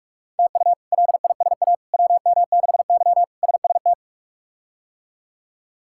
Morse code, heard as a keyed tone, about 43 wpm, 700 Hz. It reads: TU LISA WM6Y SST
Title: K1USN SST QSO Practice - 45wpm